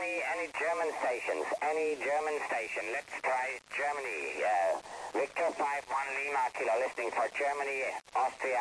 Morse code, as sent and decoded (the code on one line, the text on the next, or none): none